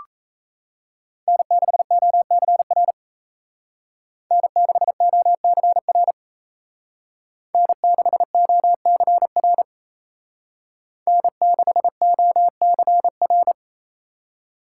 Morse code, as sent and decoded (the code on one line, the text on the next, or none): -. -.... --- -.-. .-. / -. -.... --- -.-. .-. / -. -.... --- -.-. .-. / -. -.... --- -.-. .-.
N6OCR N6OCR N6OCR N6OCR